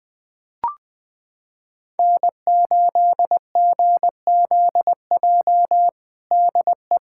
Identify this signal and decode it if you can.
Morse code, heard as a keyed tone, about 20 wpm, 700 Hz. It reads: N8GZJ DE
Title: Practice QSO - Exchange 4 - Ending - 20wpm